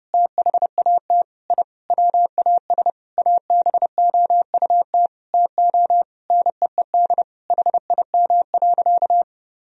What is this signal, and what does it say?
Morse code, heard as a keyed tone, about 32 wpm, 700 Hz. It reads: THAT I WAS ABOUT TO NEED HIM.
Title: Sentences from Top 1000 Words - Review Part 1 - 30wpm